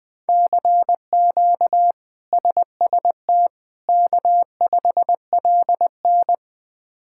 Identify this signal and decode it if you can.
Morse code, heard as a keyed tone, about 21 wpm, 700 Hz. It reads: CQ SST K5LN
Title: K1USN SST QSO Practice - 20wpm